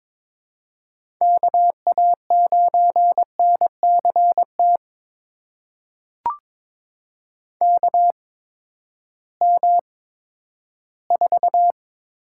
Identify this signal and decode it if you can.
Morse code, heard as a keyed tone, about 22 wpm, spaced as for 13 wpm, 700 Hz. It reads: KA9NCT KM4
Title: USA Call Signs - 6 Characters - 8x Character Spacing - 22wpm